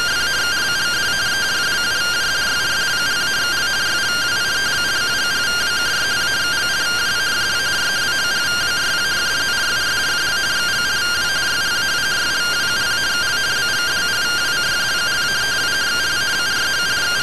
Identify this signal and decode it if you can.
Two-tone teleprinter signal, about 170 Hz shift.